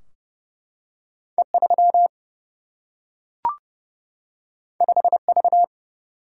Morse code, heard as a keyed tone, about 30 wpm, 700 Hz. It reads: E3 5V
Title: Single-Letter Single-Number 30wpm